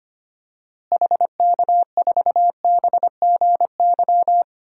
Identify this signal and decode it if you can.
Morse code, heard as a keyed tone, about 25 wpm, 700 Hz. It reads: HK4BGY
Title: DXCC-100 Call Signs - 6 Characters - Repeated 3x - 25wpm